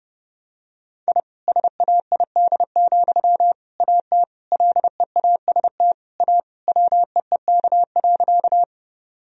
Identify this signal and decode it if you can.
Morse code, heard as a keyed tone, about 31 wpm, 700 Hz. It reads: I SAID, AT LEAST A WEEK.
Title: Sentences from Top 1000 Words - Review Part 2 - 30wpm